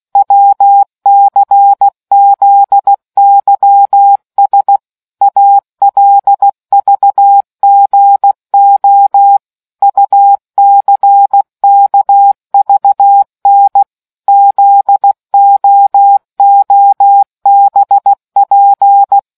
Morse code, heard as a keyed tone, about 16 wpm, 800 Hz. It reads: WCZYS ALVGO UCKVN ZOOBP